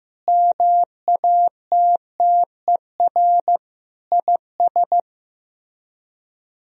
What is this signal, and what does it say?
Morse code, heard as a keyed tone, about 16 wpm, 700 Hz. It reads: MATTER IS
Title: Speed-Racing - Sets of 5 Words 15wpm